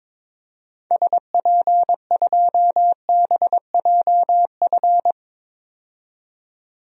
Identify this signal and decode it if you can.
Morse code, heard as a keyed tone, about 22 wpm, 700 Hz. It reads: SP2BJF
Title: DXCC-100 Call Signs - 6 Characters - 22wpm